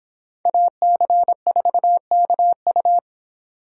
Morse code, heard as a keyed tone, about 26 wpm, 700 Hz. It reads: AC4KU